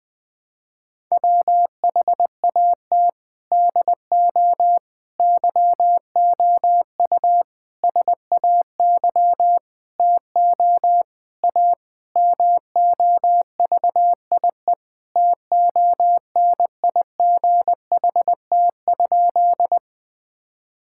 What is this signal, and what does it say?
Morse code, heard as a keyed tone, about 20 wpm, 700 Hz. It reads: WHAT DO YOU SAY TO A MOVIE TONIGHT?
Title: Sentences with 5-Word Phrases - 20wpm